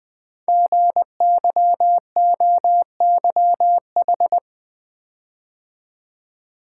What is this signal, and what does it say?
Morse code, heard as a keyed tone, about 20 wpm, 700 Hz. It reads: GYOYH
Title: Code Groups - 5 Letter:Numbers - 20wpm